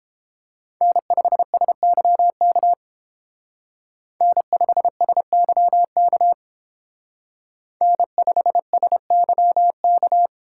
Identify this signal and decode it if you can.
Morse code, heard as a keyed tone, about 31 wpm, 700 Hz. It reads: N5SYK N5SYK N5SYK